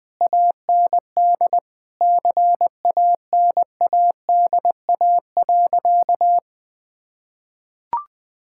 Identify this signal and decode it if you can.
Morse code, heard as a keyed tone, about 20 wpm, 700 Hz. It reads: AND CANADA.
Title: Sentences with 5-Word Phrases - Spoken Beginning - 20wpm